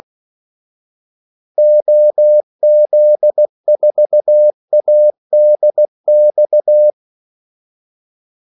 Morse code, heard as a keyed tone, about 16 wpm, 600 Hz. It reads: OZ4ADX
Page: Kaldesignaler 16 wpm